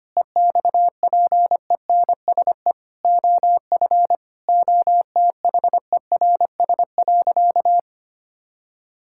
Morse code, heard as a keyed tone, about 25 wpm, 700 Hz. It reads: EXPENSE OF OTHERS.
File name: practice / 25wpm / Sentences with 4-Word Phrases - 25wpm